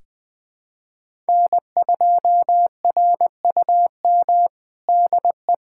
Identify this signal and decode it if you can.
Morse code, heard as a keyed tone, about 20 wpm, 700 Hz. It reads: N2RUM DE
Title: USA Callsign DE Callsign - 5 Characters - 20wpm